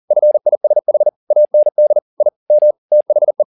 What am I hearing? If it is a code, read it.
Morse code, heard as a keyed tone, about 40 wpm, 600 Hz. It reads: FISH AND I M THE